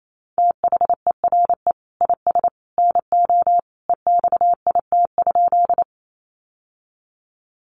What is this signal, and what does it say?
Morse code, heard as a keyed tone, about 28 wpm, 700 Hz. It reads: THERE IS NO EXIT?
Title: Sentences from Top 1100 Words 28wpm